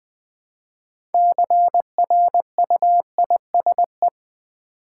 Morse code, heard as a keyed tone, about 20 wpm, 700 Hz. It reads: CRUISE
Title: 6 Letter Words - First 3 Letters Given - 20wpm